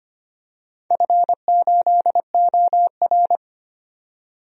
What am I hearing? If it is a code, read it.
Morse code, heard as a keyed tone, about 25 wpm, 700 Hz. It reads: F8OR
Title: DXCC-100 Call Signs - 4 Characters - 25wpm